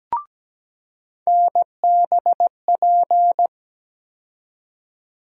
Morse code, heard as a keyed tone, about 17 wpm, 700 Hz. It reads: NBP
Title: USA Call Sign Postfixes - 3 Letters 17wpm